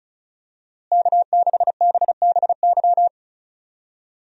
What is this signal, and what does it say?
Morse code, heard as a keyed tone, about 34 wpm, 700 Hz. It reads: K6BBY
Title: USA Call Signs - 5 Characters - Repeated 2x - 35wpm